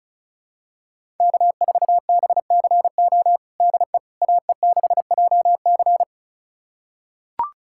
Morse code, heard as a keyed tone, about 34 wpm, 700 Hz. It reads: K4BCO DE AE6JC
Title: USA Callsign DE Callsign - 5 Characters - 35wpm